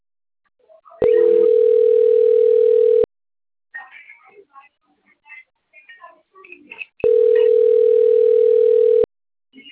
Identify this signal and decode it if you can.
Ringback tone (North America), 440 + 480 Hz.